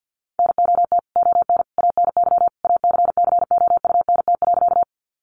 Morse code, heard as a keyed tone, about 50 wpm, 700 Hz. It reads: NOT ON ANY ACCOUNT.
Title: Sentences from Top 900 Words 50wpm